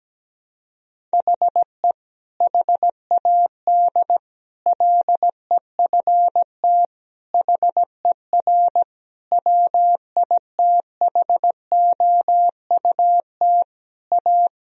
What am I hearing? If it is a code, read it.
Morse code, heard as a keyed tone, about 17 wpm, 700 Hz. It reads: HE HAD LEFT HER WITHOUT A
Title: Sentences from Top 1000 Words - Review Part 3 - 17wpm